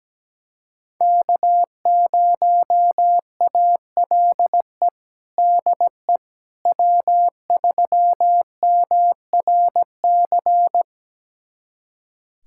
Morse code, heard as a keyed tone, about 17 wpm, 700 Hz. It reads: K0ALE DE W3MRC
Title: USA Callsign DE Callsign - 5 Characters - 17wpm